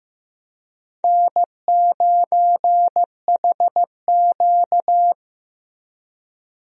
Morse code, heard as a keyed tone, about 15 wpm, 700 Hz. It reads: N9HQ
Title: USA Call Signs - 4 Characters - Repeated 3x - 15wpm